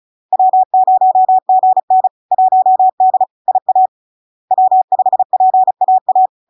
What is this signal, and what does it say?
Morse code, heard as a keyed tone, about 36 wpm, 750 Hz. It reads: W0GN 1D IA W5PAA
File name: LARK-FD-2022-Log-in-CW-35wpm-750-Hz